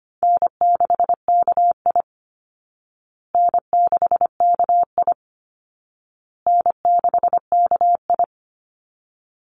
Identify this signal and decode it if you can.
Morse code, heard as a keyed tone, about 25 wpm, 700 Hz. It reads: N6KI N6KI N6KI
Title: Easy Call Signs Repeated 3x - 25wpm